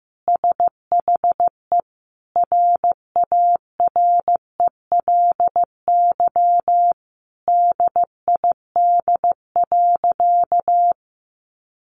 Morse code, heard as a keyed tone, about 15 wpm, 700 Hz. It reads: SHE RARELY DID.
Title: Sentences from Top 1400 Words 15wpm